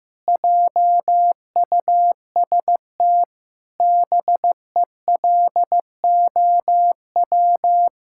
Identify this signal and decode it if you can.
Morse code, heard as a keyed tone, about 16 wpm, 700 Hz. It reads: JUST BELOW
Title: Sentences from Top 1000 Words - Review Part 5 - 15wpm